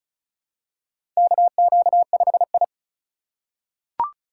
Morse code, heard as a keyed tone, about 36 wpm, 700 Hz. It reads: KQ5I
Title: USA Call Signs - 4 Characters - 3x Character Spacing - 35wpm